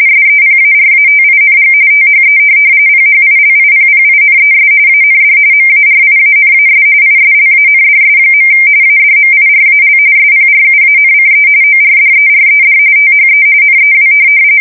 RTTY, about 45.45 baud. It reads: SEPARATING THE PHILIPPINES AND MARIANAS ISLANDS. ADMIRAL SPRUANCE ORDERED THE NEW
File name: fsk170 / USS-New-Jersey